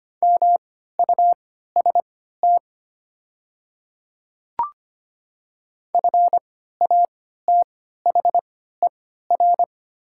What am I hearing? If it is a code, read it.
Morse code, heard as a keyed tone, about 25 wpm, spaced as for 14 wpm, 700 Hz. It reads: MUST FATHER
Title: 3x Character Spacing - Top 300 Words - 25wpm